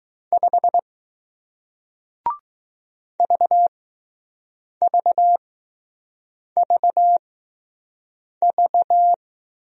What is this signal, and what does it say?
Morse code, heard as a keyed tone, about 20 wpm, spaced as for 6 wpm, 700 Hz. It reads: 5 VVVV